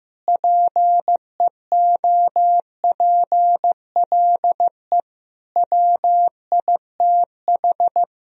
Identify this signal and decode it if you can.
Morse code, heard as a keyed tone, about 15 wpm, 700 Hz. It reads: PEOPLE WITH
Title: Sentences from Top 1000 Words - Review Part 1 - 15wpm